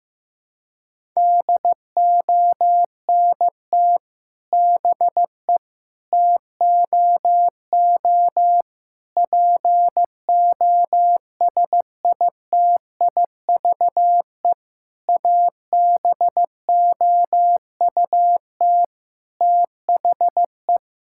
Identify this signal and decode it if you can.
Morse code, heard as a keyed tone, about 15 wpm, 700 Hz. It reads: DONT BE TOO POSITIVE ABOUT THE
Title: Sentences from Top 1000 Words - Review Part 5 - 15wpm